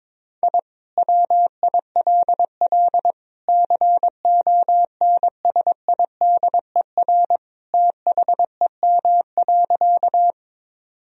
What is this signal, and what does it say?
Morse code, heard as a keyed tone, about 22 wpm, 700 Hz. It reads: I WILL CONSIDER THEM.
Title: Sentences from Top 1000 Words - Review Part 4 - 22wpm